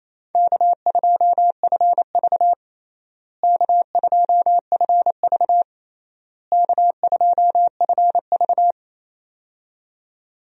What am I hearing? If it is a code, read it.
Morse code, heard as a keyed tone, about 27 wpm, 700 Hz. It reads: K2FV K2FV K2FV
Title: USA Call Signs - 4 Characters - Repeated 3x - 28wpm